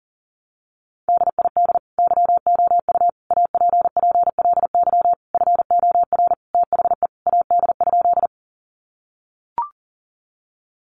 Morse code, heard as a keyed tone, about 40 wpm, 700 Hz. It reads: DID YOU APPLY FOR THE AD?
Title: Sentences from Top 1300 Words 40wpm